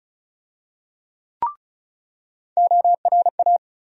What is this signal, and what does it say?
Morse code, heard as a keyed tone, about 36 wpm, 700 Hz. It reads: ORA
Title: USA Call Sign Postfixes - 3 Letters 35wpm